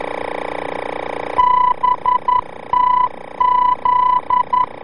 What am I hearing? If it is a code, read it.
Morse code, heard as a keyed tone, about 11 wpm, 1000 Hz. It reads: BTZ
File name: BTZ-VOR-F